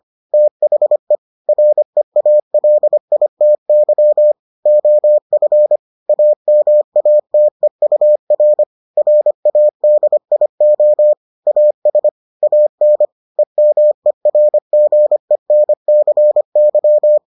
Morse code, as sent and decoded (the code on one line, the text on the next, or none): - .... / .-. .- .-.. .. - -.-- / --- ..-. / .- -- .- - . ..- .-. / .-. .- -.. .. --- / .- ... / .- -. / . -- . .-. --. -. -.-. -.--
THE REALITY OF AMATEUR RADIO AS AN EMERGENCY